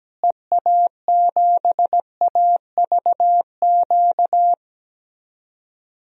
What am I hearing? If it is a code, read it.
Morse code, heard as a keyed tone, about 17 wpm, 700 Hz. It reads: EA7AVQ
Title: DXCC-100 Call Signs - 6 Characters - Repeated 3x - 17wpm